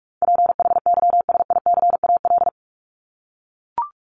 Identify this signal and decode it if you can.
Morse code, heard as a keyed tone, about 44 wpm, 700 Hz. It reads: PHYSICAL
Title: Top 501-750 Words 45wpm